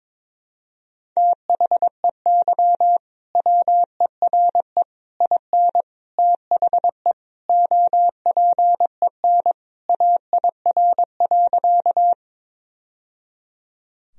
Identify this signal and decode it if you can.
Morse code, heard as a keyed tone, about 22 wpm, 700 Hz. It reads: THEY WERE IN THE OPEN AIR.
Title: Sentences from Top 1000 Words - Review Part 1 - 22wpm